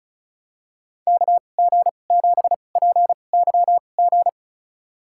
Morse code, heard as a keyed tone, about 34 wpm, spaced as for 25 wpm, 700 Hz. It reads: KG7PYG
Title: USA Call Signs - 6 Characters - 2x Character Spacing - 35wpm